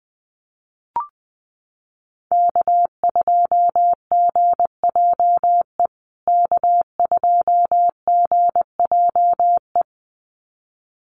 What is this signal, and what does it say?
Morse code, heard as a keyed tone, about 20 wpm, 700 Hz. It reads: K2GJE K2GJE
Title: USA Call Signs Repeated 2x 20wpm